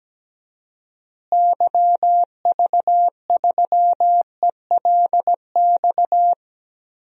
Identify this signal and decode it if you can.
Morse code, heard as a keyed tone, about 17 wpm, 700 Hz. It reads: YV3ELX
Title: DXCC-100 Call Signs - 6 Characters - Repeated 2x - 17wpm